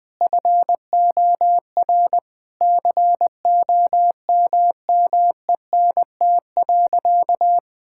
Morse code, heard as a keyed tone, about 20 wpm, 700 Hz. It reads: FOR COMMENT.
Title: Sentences with 6-Word Phrases - 20wpm